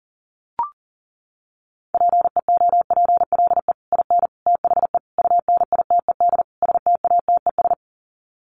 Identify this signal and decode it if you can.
Morse code, heard as a keyed tone, about 40 wpm, 700 Hz. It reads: PEOPLE IN THE UNITED STATES
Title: Sets of 5 Words Encore 40wpm